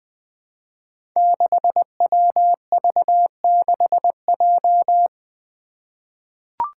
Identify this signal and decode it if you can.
Morse code, heard as a keyed tone, about 21 wpm, 700 Hz. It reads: 6WV6J
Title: Five Challenging Letter-Number Combinations 20wpm